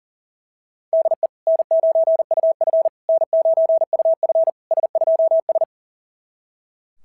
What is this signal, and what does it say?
Morse code, heard as a keyed tone, about 42 wpm, 690 Hz. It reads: DE N9UF N9UF S2S
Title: SOTA QSO Practice 40wpm